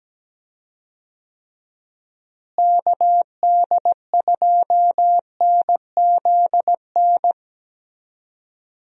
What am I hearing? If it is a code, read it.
Morse code, heard as a keyed tone, about 17 wpm, 700 Hz. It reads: KD2NZN
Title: USA Call Signs - 6 Characters - Repeated 2x - 17wpm